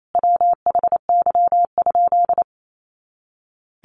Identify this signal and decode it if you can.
Morse code, heard as a keyed tone, about 28 wpm, 700 Hz. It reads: WHY?
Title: Sentences from Top 200 Words 28wpm